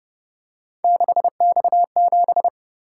Morse code, heard as a keyed tone, about 29 wpm, 700 Hz. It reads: 6X7